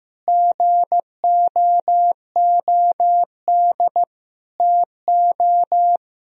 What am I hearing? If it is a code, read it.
Morse code, heard as a keyed tone, about 15 wpm, 700 Hz. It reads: GOOD TO
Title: Sentences from Top 1000 Words - Review Part 1 - 15wpm